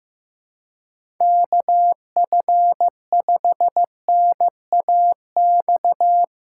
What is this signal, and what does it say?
Morse code, heard as a keyed tone, about 15 wpm, 700 Hz. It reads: KF5NAX
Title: USA Call Signs - 6 Characters - 3x Character Spacing - 15wpm